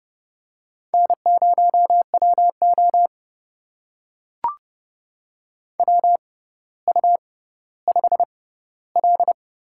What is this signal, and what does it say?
Morse code, heard as a keyed tone, about 30 wpm, spaced as for 15 wpm, 700 Hz. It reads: N0WO WU5L
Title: USA Call Signs - 4 Characters - 6x Character Spacing - 30wpm